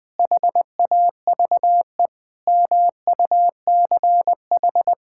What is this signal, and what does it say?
Morse code, heard as a keyed tone, about 20 wpm, 700 Hz. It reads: HAVE MUCH
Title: Sentences from Top 1000 Words - Review Part 5 - 20wpm